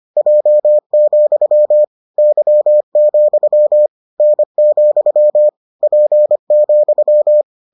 Morse code, heard as a keyed tone, about 25 wpm, 600 Hz. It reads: J, Y, N, P,